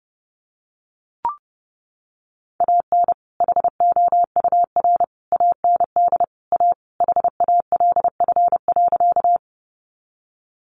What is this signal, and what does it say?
Morse code, heard as a keyed tone, about 30 wpm, 700 Hz. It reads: AN HOUR AND A HALF.
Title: Sentences from Top 400 Words 30wpm